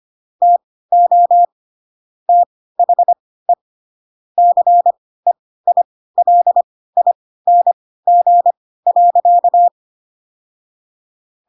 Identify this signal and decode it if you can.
Morse code, heard as a keyed tone, about 25 wpm, spaced as for 16 wpm, 700 Hz. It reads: TO THE CEILING.
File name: Pinocchio / Pinocchio ch1 10 25 ik1hge